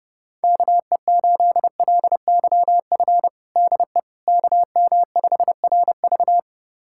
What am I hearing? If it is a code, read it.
Morse code, heard as a keyed tone, about 30 wpm, 700 Hz. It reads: KE8LYF DE KM5RV
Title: USA Callsign DE Callsign 30wpm